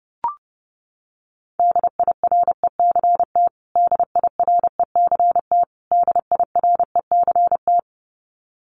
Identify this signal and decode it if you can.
Morse code, heard as a keyed tone, about 30 wpm, 700 Hz. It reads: DIRECT DIRECT DIRECT
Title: Top 500 Words Repeated 30wpm